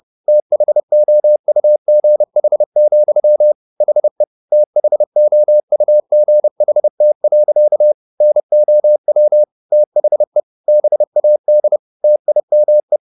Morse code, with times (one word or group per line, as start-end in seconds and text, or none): 0.3-3.5 THOUGH,
3.8-4.2 HE
4.5-7.9 THOUGHT.
8.2-9.4 NOW
9.7-10.4 THE
10.7-11.8 BAD
12.0-13.0 TIME